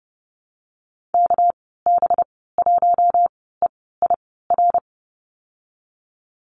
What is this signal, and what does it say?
Morse code, heard as a keyed tone, about 30 wpm, spaced as for 17 wpm, 700 Hz. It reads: KB1EIR
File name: Lesson 37g - Letter x - Call Signs 30wpm